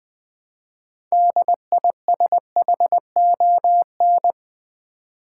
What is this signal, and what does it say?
Morse code, heard as a keyed tone, about 20 wpm, 700 Hz. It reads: DISHON
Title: Build A Word - 3 Letters At A Time - 9-Letter Words - 20wpm